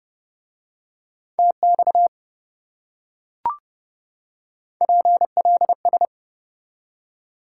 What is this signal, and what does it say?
Morse code, heard as a keyed tone, about 29 wpm, 700 Hz. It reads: TX PLS